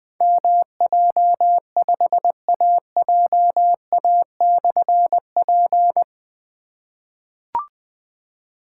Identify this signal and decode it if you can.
Morse code, heard as a keyed tone, about 20 wpm, 700 Hz. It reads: MJ5AJA/P
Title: Difficult Call Signs 20wpm